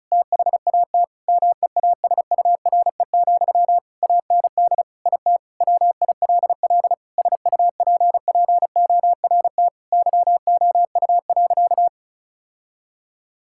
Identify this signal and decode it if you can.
Morse code, heard as a keyed tone, about 36 wpm, 700 Hz. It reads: THAT MEASURE, AND IT WILL SUPPORT YOU.
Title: Sentences from Top 1000 Words - Review Part 5 - 35wpm